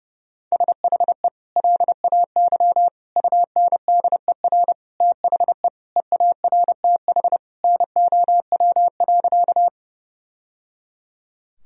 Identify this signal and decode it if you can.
Morse code, heard as a keyed tone, about 30 wpm, 700 Hz. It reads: SHE LAY UNDER THE EARTH NOW.
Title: Sentences from Top 1000 Words - Review Part 2 - 30wpm